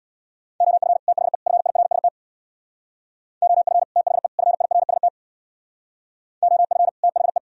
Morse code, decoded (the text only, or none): OF THE FIRST OF THE FIRST OF THE